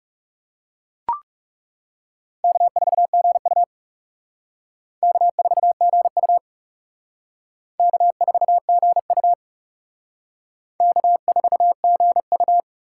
Morse code, heard as a keyed tone, about 38 wpm, 700 Hz. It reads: K4GU K4GU K4GU K4GU